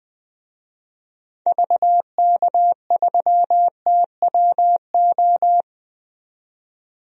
Morse code, heard as a keyed tone, about 20 wpm, 700 Hz. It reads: VK3TWO
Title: DXCC-100 Call Signs - 6 Characters - 20wpm